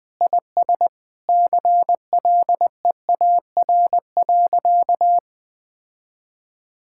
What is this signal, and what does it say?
Morse code, heard as a keyed tone, about 20 wpm, 700 Hz. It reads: IS CLEAR.
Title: Sentences with 6-Word Phrases - Spoken Beginning - 20wpm